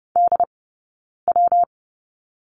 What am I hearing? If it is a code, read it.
Morse code, heard as a keyed tone, about 30 wpm, 700 Hz. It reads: D W
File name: Lesson 19d - Letter w - Two Characters 30wpm